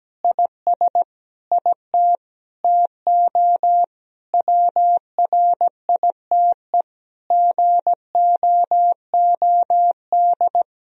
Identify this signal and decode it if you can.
Morse code, heard as a keyed tone, about 17 wpm, 700 Hz. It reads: IS IT TO WRITE GOOD
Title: Sentences from Top 1000 Words - Review Part 3 - 17wpm